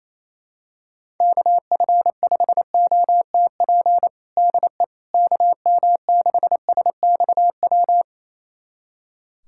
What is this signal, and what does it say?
Morse code, heard as a keyed tone, about 28 wpm, 700 Hz. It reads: KF5OTP DE KM6SXW
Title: USA Callsign DE Callsign 28wpm